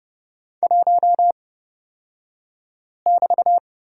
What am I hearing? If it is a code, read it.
Morse code, heard as a keyed tone, about 31 wpm, spaced as for 5 wpm, 700 Hz. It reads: 1<BT>
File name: Lesson 36d - BT - Two Characters 30wpm